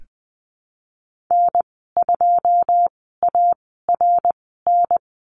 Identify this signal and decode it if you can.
Morse code, heard as a keyed tone, about 20 wpm, spaced as for 14 wpm, 700 Hz. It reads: N2ARN
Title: 088 Lesson 15g - Number 5 - Call Signs 20wpm